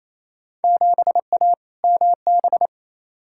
Morse code, heard as a keyed tone, about 28 wpm, 700 Hz. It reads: 7A MB
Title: Field Day Exchanges Encore - 28wpm